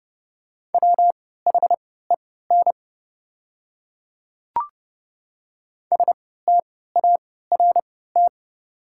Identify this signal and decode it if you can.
Morse code, heard as a keyed tone, about 30 wpm, spaced as for 17 wpm, 700 Hz. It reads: WHEN START
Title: 3x Character Spacing - Top 300 Words - 30wpm